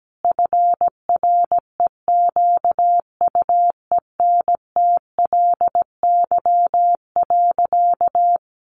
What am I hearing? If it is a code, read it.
Morse code, heard as a keyed tone, about 17 wpm, 700 Hz. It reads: FREQUENTLY.
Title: Sentences from Top 1300 Words 17wpm